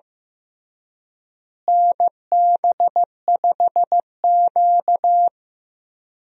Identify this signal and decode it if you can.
Morse code, heard as a keyed tone, about 15 wpm, 700 Hz. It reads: NB5Q
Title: USA Call Signs - 4 Characters - Repeated 3x - 15wpm